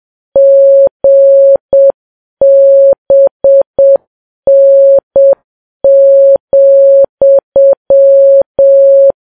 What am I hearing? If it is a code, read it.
Morse code, heard as a keyed tone, about 7 wpm, 550 Hz. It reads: GBN,
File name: Files Web tls / Morse on9fv / morse7kps